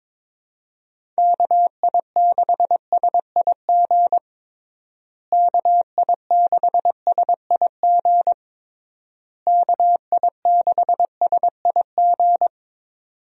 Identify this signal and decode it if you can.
Morse code, heard as a keyed tone, about 22 wpm, 700 Hz. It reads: KI6SIG KI6SIG KI6SIG
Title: USA Call Signs - 6 Characters - Repeated 3x - 22wpm